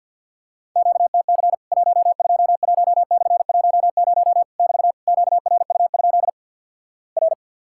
Morse code, heard as a keyed tone, about 50 wpm, 700 Hz. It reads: QTC 121/10 <BT> QRU? R
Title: Worked All Europe DX Contest - QTC Exchange - 50wpm